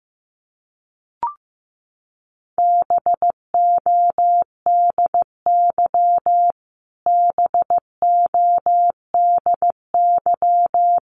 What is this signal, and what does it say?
Morse code, heard as a keyed tone, about 15 wpm, 700 Hz. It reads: BODY BODY
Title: Top 500 Words Repeated 15wpm